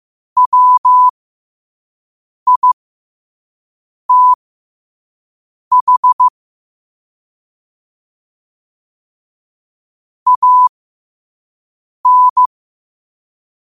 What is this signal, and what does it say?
Morse code, heard as a keyed tone, about 15 wpm, spaced as for 5 wpm, 1000 Hz. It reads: WITH AN
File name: qotdfiles / 5wpm